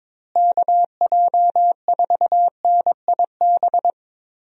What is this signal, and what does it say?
Morse code, heard as a keyed tone, about 22 wpm, 700 Hz. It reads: KJ4NIB
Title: USA Call Signs - 6 Characters - Repeated 3x - 22wpm